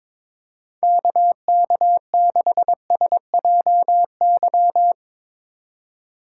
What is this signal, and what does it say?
Morse code, heard as a keyed tone, about 22 wpm, 700 Hz. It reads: KK6SJY
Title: USA Call Signs - 6 Characters - 2x Character Spacing - 22wpm